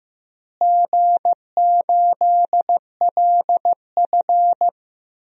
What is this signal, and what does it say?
Morse code, heard as a keyed tone, about 16 wpm, 700 Hz. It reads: G8LF
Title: DXCC-100 Call Signs - 4 Characters - Repeated 3x - 15wpm